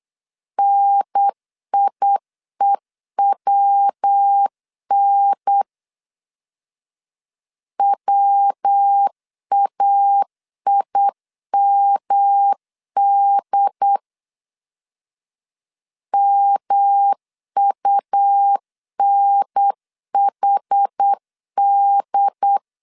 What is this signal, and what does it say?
Morse code, heard as a keyed tone, about 8 wpm, 800 Hz. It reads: NIEWN WAIMD MUNHD
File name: CW / DET / 1 3 1500C S B 8wpm 4141